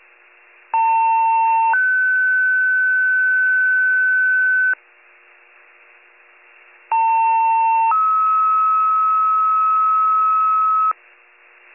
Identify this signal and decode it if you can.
Two-tone page, heard as a one-second tone, then a three-second tone, about 900 then 1500 Hz, about 900 then 1290 Hz, with noise. It is heard from 0.7 to 10.9 s.